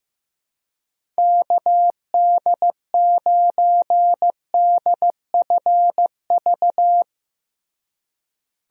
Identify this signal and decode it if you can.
Morse code, heard as a keyed tone, about 15 wpm, 700 Hz. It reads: KD9DFV
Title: USA Call Signs - 6 Characters - Repeated 3x - 15wpm